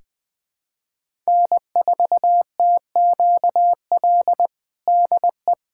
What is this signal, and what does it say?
Morse code, heard as a keyed tone, about 20 wpm, 700 Hz. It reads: N4TQL DE
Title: USA Callsign DE Callsign - 5 Characters - 20wpm